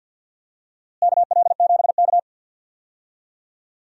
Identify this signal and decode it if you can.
Morse code, heard as a keyed tone, about 51 wpm, 700 Hz. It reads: KL7K
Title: DXCC-100 Call Signs - 4 Characters - Repeated 2x - 50wpm